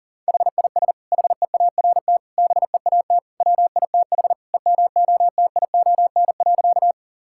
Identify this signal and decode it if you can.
Morse code, heard as a keyed tone, about 40 wpm, 700 Hz. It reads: HIS HEART BEAT WITH EMOTION.
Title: Sentences from Top 1600 Words 40wpm